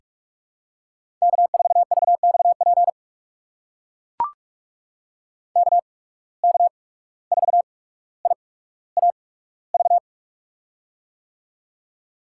Morse code, heard as a keyed tone, about 44 wpm, spaced as for 18 wpm, 700 Hz. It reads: K4VXP KK4IAV
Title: USA Call Signs - 8x Character Spacing - 45wpm